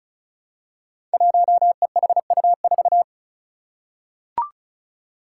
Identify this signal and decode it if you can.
Morse code, heard as a keyed tone, about 36 wpm, 700 Hz. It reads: 1EHU4
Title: Five Challenging Letter-Number Combinations 35wpm